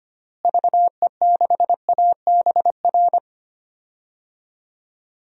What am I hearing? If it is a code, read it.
Morse code, heard as a keyed tone, about 25 wpm, 700 Hz. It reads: VE6ABR
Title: DXCC-100 Call Signs - 6 Characters - Repeated 2x - 25wpm